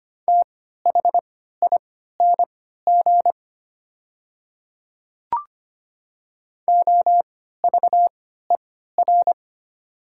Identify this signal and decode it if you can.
Morse code, heard as a keyed tone, about 25 wpm, spaced as for 14 wpm, 700 Hz. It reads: THING OVER
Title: 3x Character Spacing - Top 100 Words - 25wpm